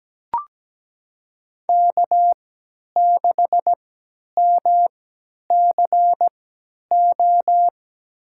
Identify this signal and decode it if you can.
Morse code, heard as a keyed tone, about 17 wpm, spaced as for 10 wpm, 700 Hz. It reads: K6MCO
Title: USA Call Signs - 5 Characters - 3x Character Spacing - 17wpm